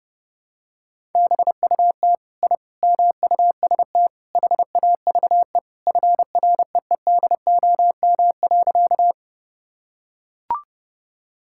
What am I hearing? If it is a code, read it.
Morse code, heard as a keyed tone, about 30 wpm, 700 Hz. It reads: BUT I MUST HAVE FREEDOM.
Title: Sentences from Top 1000 Words - Review Part 5 - 30wpm